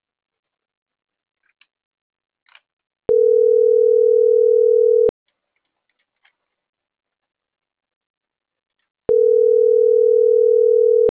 Ringback tone (North America), 440 + 480 Hz.